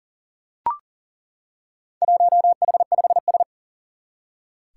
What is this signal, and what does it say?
Morse code, heard as a keyed tone, about 40 wpm, 700 Hz. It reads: 1H5S
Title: Four Challenging Letter-Number Combinations 40wpm